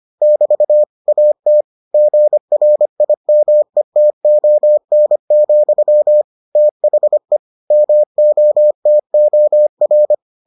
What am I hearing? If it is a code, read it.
Morse code, heard as a keyed tone, about 25 wpm, 600 Hz. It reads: <BT> AT GRIMETON, THE MOTOR